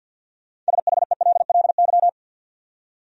Morse code, heard as a keyed tone, about 46 wpm, 700 Hz. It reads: SHELLY
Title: USA Common Names - 50wpm